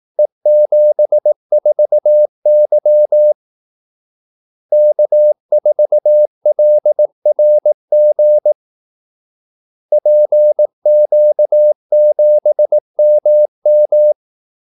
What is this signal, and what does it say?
Morse code, heard as a keyed tone, about 18 wpm, 600 Hz. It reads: E74Y K4LRG PQ7MM
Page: Kaldesignaler 18 wpm